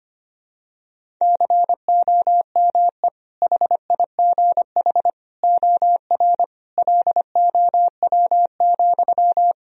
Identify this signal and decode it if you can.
Morse code, heard as a keyed tone, about 25 wpm, 700 Hz. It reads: COME HIGH OR LOW,
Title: Sentences from Top 1000 Words - Review Part 1 - 25wpm